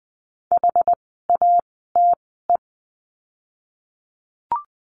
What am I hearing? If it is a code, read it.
Morse code, heard as a keyed tone, about 20 wpm, spaced as for 14 wpm, 700 Hz. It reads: HATE
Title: 068 Lesson 12f - Letter d - Words 20wpm